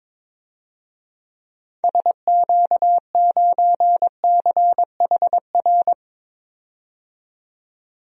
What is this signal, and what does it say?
Morse code, heard as a keyed tone, about 21 wpm, 700 Hz. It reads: SQ9CHR
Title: DXCC-100 Call Signs - 6 Characters - 22wpm